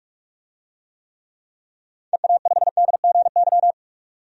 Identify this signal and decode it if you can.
Morse code, heard as a keyed tone, about 41 wpm, 700 Hz. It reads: EA5DGY
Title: DXCC-100 Call Signs - 6 Characters - 45wpm